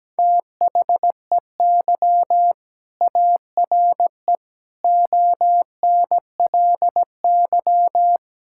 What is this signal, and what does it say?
Morse code, heard as a keyed tone, about 17 wpm, 700 Hz. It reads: THEY ARE ONLY
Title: Sentences from Top 1000 Words - Review Part 6 - 17wpm